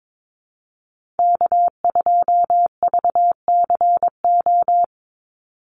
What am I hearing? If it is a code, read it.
Morse code, heard as a keyed tone, about 22 wpm, 700 Hz. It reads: K2VCO
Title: Call Signs - 22wpm